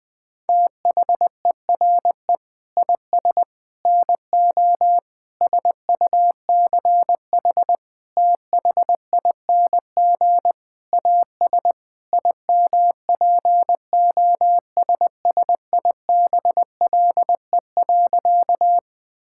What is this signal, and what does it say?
Morse code, heard as a keyed tone, about 21 wpm, 700 Hz. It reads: THERE IS NO SUCH THING AS IMPOSSIBLE.
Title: Sentences with 6-Word Phrases - Spoken Beginning - 20wpm